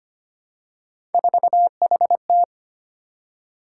Morse code, heard as a keyed tone, about 25 wpm, 700 Hz. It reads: 4HT
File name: Three Letter-Number Combinations - 25wpm